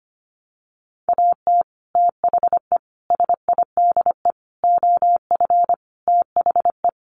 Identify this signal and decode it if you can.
Morse code, heard as a keyed tone, about 25 wpm, 700 Hz. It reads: AT THE SIDE OF THE
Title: Sets of 5 Words Encore2 25wpm